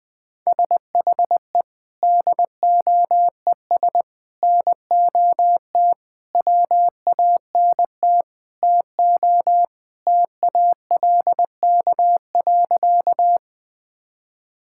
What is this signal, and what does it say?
Morse code, heard as a keyed tone, about 20 wpm, 700 Hz. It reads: SHE DOES NOT WANT TO TALK.
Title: Sentences with 4-Word Phrases - Spoken Beginning - 20wpm